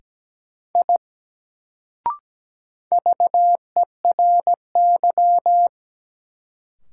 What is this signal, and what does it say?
Morse code, heard as a keyed tone, about 17 wpm, 700 Hz. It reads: I VERY
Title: Sentences from Top 200 Words - New Word Drill - 17wpm